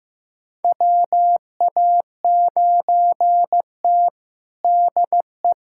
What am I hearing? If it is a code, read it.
Morse code, heard as a keyed tone, about 15 wpm, 700 Hz. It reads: WA9T DE
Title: USA Callsign DE Callsign - 4 Characters - 15wpm